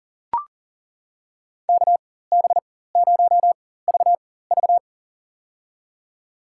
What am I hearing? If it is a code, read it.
Morse code, heard as a keyed tone, about 40 wpm, spaced as for 19 wpm, 700 Hz. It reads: KB0VV
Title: USA Call Signs - 4x Character Spacing - 40wpm